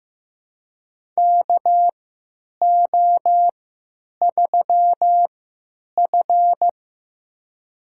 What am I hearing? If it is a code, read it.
Morse code, heard as a keyed tone, about 15 wpm, spaced as for 9 wpm, 700 Hz. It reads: KO3F
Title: USA Call Signs - 4 Characters - 3x Character Spacing - 15wpm